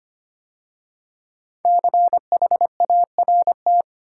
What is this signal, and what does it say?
Morse code, heard as a keyed tone, about 25 wpm, 700 Hz. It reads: CHART
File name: 5 Letter Words - First 3 Letters Given - 25wpm